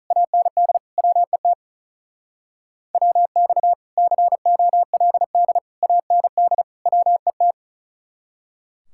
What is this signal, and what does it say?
Morse code, heard as a keyed tone, about 44 wpm, 700 Hz. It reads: AND WET WX COLD AND WET